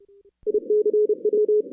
Morse code, heard as a keyed tone, about 29 wpm, 410 Hz. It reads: ICW
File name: 2016 arrl dx cw / 2016-ARRL-DX-CW-K5TR- - 0-46